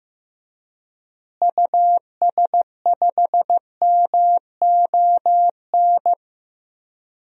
Morse code, heard as a keyed tone, about 15 wpm, 700 Hz. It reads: US5MON
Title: DXCC-100 Call Signs - 6 Characters - Repeated 3x - 15wpm